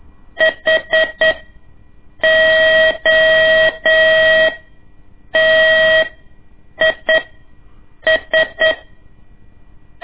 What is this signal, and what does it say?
Morse code, heard as a keyed tone, about 8 wpm, spaced as for 6 wpm, 640 Hz. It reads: HOTIS
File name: ik1weg2 / CORSI CW / lez2-1